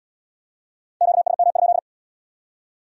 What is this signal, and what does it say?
Morse code, heard as a keyed tone, about 77 wpm, 700 Hz. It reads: QSM?